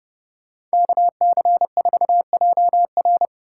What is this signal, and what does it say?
Morse code, heard as a keyed tone, about 38 wpm, 700 Hz. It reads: KC4JR